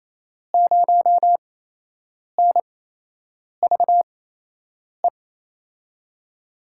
Morse code, heard as a keyed tone, about 28 wpm, spaced as for 8 wpm, 700 Hz. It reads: 0NVE
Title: USA Call Signs - 6 Characters - 8x Character Spacing - 28wpm